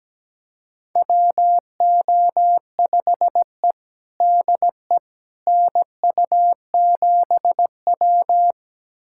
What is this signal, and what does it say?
Morse code, heard as a keyed tone, about 17 wpm, 700 Hz. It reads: WO5E DE NU7W
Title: USA Callsign DE Callsign - 4 Characters - 17wpm